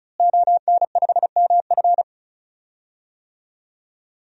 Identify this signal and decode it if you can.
Morse code, heard as a keyed tone, about 36 wpm, 700 Hz. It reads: ON5MF
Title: DXCC-100 Call Signs - 5 Characters - Repeated 3x - 35wpm